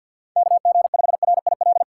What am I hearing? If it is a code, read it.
Morse code, heard as a keyed tone, about 48 wpm, 700 Hz. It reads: KG5RIL
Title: USA Call Signs - 6 Characters - 3x Character Spacing - 50wpm